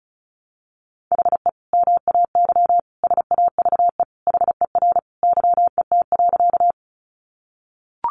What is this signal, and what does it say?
Morse code, heard as a keyed tone, about 34 wpm, 700 Hz. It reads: HE MAY SAVE HER YET.
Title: Sentences from Top 1100 Words 35wpm